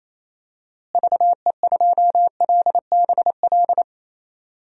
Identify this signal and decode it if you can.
Morse code, heard as a keyed tone, about 27 wpm, 700 Hz. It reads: VE2LBL
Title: DXCC-100 Call Signs - 6 Characters - Repeated 3x - 28wpm